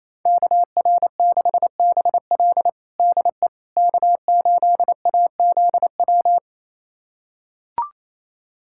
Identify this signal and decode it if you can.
Morse code, heard as a keyed tone, about 28 wpm, 700 Hz. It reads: KR6BL DE K8AZW
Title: USA Callsign DE Callsign - 5 Characters - 28wpm